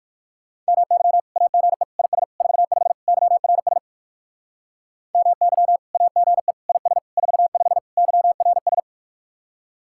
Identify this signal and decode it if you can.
Morse code, heard as a keyed tone, about 60 wpm, 700 Hz. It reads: MY AGE IS 45 YRS MY AGE IS 45 YRS